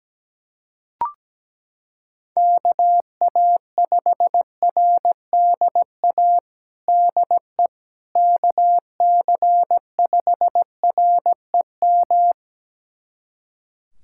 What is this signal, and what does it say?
Morse code, heard as a keyed tone, about 17 wpm, 700 Hz. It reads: KA5RDA DE KC5REM
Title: USA Callsign DE Callsign 17wpm